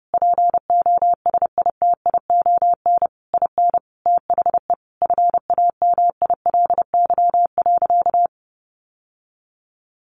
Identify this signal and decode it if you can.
Morse code, heard as a keyed tone, about 30 wpm, 700 Hz. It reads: POSITION IN THE FAMILY.
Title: Sentences from Top 1200 Words 30wpm